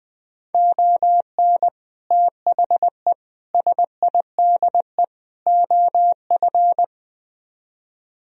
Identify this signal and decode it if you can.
Morse code, heard as a keyed tone, about 21 wpm, 700 Hz. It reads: ON THE SIDE OF
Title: Speed-Racing - Sets of 4 Words 20wpm